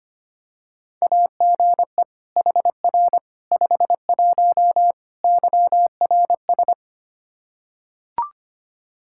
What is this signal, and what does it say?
Morse code, heard as a keyed tone, about 25 wpm, 700 Hz. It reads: AGE HR 51 YRS
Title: Practice QSO - Exchange 3 - More Info - 25wpm